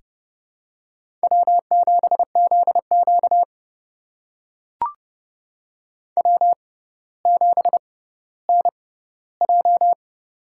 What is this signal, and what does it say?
Morse code, heard as a keyed tone, about 30 wpm, spaced as for 15 wpm, 700 Hz. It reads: W7ZQ W7NJ
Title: USA Call Signs - 6x Character Spacing - 30wpm